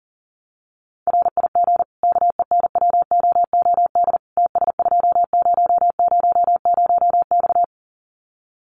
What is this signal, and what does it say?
Morse code, heard as a keyed tone, about 40 wpm, 700 Hz. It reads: RIG KENWOOD TS2000X
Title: Common QSO Phrases 40wpm